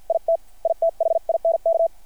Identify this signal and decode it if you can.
Morse code, heard as a keyed tone, about 50 wpm, spaced as for 43 wpm, 660 Hz. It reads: IT I THINK
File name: archived / CRSnet